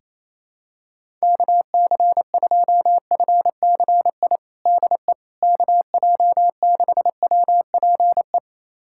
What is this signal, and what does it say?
Morse code, heard as a keyed tone, about 28 wpm, 700 Hz. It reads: KC2FCI DE KJ6WPE
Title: USA Callsign DE Callsign 28wpm